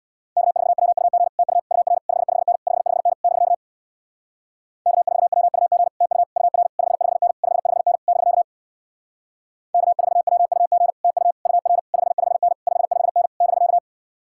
Morse code, heard as a keyed tone, about 64 wpm, 700 Hz. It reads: K4WUG TU UR 55N 55N <BK> K4WUG TU UR 55N 55N <BK> K4WUG TU UR 55N 55N <BK>